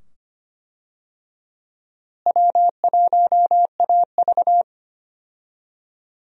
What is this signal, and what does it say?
Morse code, heard as a keyed tone, about 25 wpm, 700 Hz. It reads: W1AV
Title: USA Call Signs - 4 Characters - Repeated 2x - 25wpm